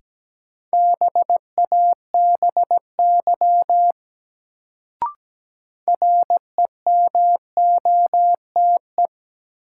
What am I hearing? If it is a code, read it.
Morse code, heard as a keyed tone, about 17 wpm, 700 Hz. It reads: BABY REMOTE
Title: Sentences from Top 1300 Words - New Word Drill Alt - 17wpm